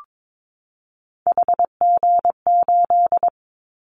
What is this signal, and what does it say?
Morse code, heard as a keyed tone, about 22 wpm, 700 Hz. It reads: HG8
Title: Callsign Prefixes for Top 25 DXCC - 22wpm